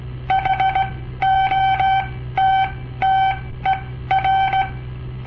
Morse code, heard as a keyed tone, about 17 wpm, spaced as for 13 wpm, 770 Hz. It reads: HOTTER
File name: papertapes / local / Code Practice-01